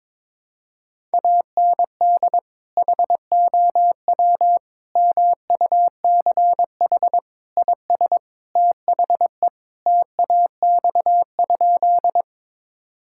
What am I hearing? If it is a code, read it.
Morse code, heard as a keyed tone, about 22 wpm, 700 Hz. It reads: AND HOW MUCH IS THE TAX?
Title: Sentences from Top 1000 Words - Review Part 5 - 22wpm